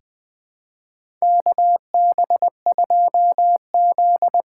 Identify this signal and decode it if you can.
Morse code, heard as a keyed tone, about 20 wpm, 700 Hz. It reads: KB2Z
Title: USA Call Signs - 6x Character Spacing - 20wpm